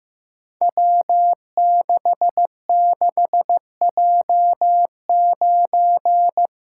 Morse code, heard as a keyed tone, about 15 wpm, 700 Hz. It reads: W66J9
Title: Code Groups - 5 Letter:Numbers - 15wpm